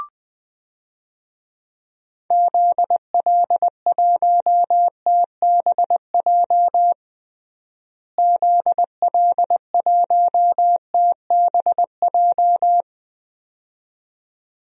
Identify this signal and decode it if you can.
Morse code, heard as a keyed tone, about 20 wpm, 700 Hz. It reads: ZL1TBJ ZL1TBJ
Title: DXCC-100 Call Signs - 6 Characters - Repeated 2x - 20wpm